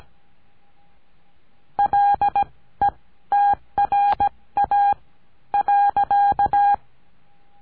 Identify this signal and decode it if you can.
Morse code, heard as a keyed tone, about 17 wpm, spaced as for 13 wpm, 810 Hz. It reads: LETRA.
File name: CURSO DE TELEGRAFIA